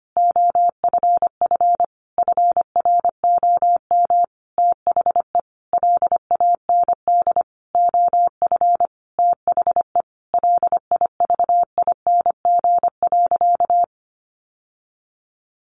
Morse code, heard as a keyed tone, about 25 wpm, 700 Hz. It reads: OFF FROM THE LAND OF THE LIVING.
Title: Sentences from Top 1100 Words 25wpm